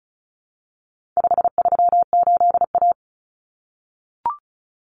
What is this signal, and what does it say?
Morse code, heard as a keyed tone, about 34 wpm, 700 Hz. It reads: 538A
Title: Common Cut Numbers - 4-Digit 35wpm